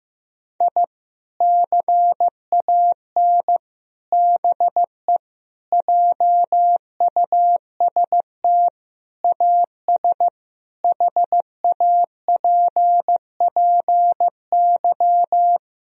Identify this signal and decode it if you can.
Morse code, heard as a keyed tone, about 15 wpm, 700 Hz. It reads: I CAN BE JUST AS HAPPY
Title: Sentences from Top 1000 Words - Review Part 3 - 15wpm